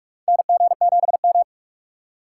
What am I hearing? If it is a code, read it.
Morse code, heard as a keyed tone, about 43 wpm, 700 Hz. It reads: NG7M
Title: USA Call Signs - 4 Characters - 2x Character Spacing - 45wpm